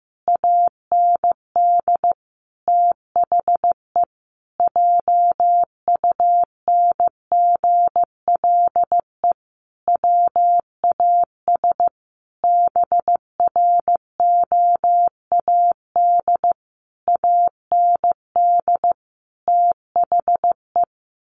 Morse code, heard as a keyed tone, about 15 wpm, 700 Hz. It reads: AND THE JUNGLE WAS BROAD AND THE
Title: A Sound Of Thunder 15wpm